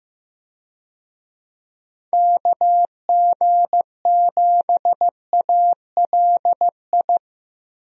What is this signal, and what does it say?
Morse code, heard as a keyed tone, about 15 wpm, 700 Hz. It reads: KG7ALI
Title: USA Call Signs - 6 Characters - Repeated 2x - 15wpm